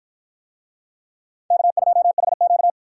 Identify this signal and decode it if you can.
Morse code, heard as a keyed tone, about 51 wpm, 700 Hz. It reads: K2HQ